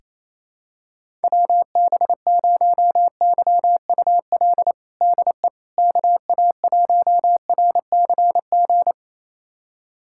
Morse code, heard as a keyed tone, about 28 wpm, 700 Hz. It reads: WB0YUL DE KA1RCG
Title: USA Callsign DE Callsign 28wpm